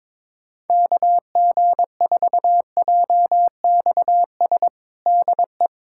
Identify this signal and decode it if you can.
Morse code, heard as a keyed tone, about 22 wpm, 700 Hz. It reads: KG4JXS DE
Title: USA Callsign DE Callsign - 6 Characters - 22wpm